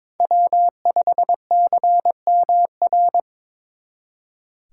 Morse code, heard as a keyed tone, about 26 wpm, 700 Hz. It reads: W5CMR
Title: Speed Racing - USA 1x3 Call Signs 22wpm